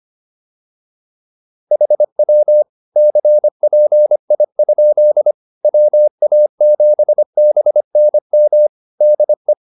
Morse code, heard as a keyed tone, about 25 wpm, 600 Hz. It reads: HW CPI? WA7BNM DE